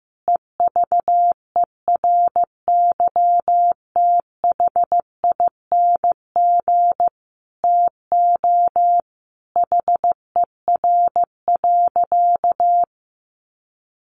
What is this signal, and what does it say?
Morse code, heard as a keyed tone, about 14 wpm, 700 Hz. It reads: EVERYTHING TO HER.
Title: Sentences from Top 700 Words 15wpm